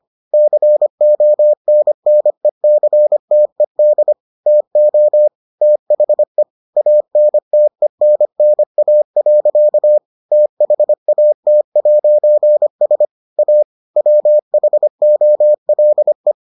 Morse code, heard as a keyed tone, about 25 wpm, 600 Hz. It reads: CONNECTED TO THE ANTENNA. THAT'S A WHOLE